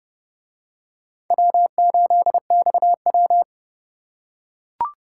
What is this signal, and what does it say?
Morse code, heard as a keyed tone, about 28 wpm, 700 Hz. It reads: W8XW